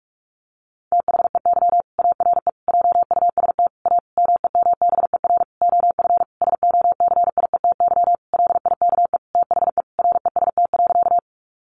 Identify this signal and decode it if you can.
Morse code, heard as a keyed tone, about 44 wpm, 700 Hz. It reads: THEY ARE JUST A MEMBER OF SOCIETY LIKE THE REST.
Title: Sentences from Top 1100 Words 45wpm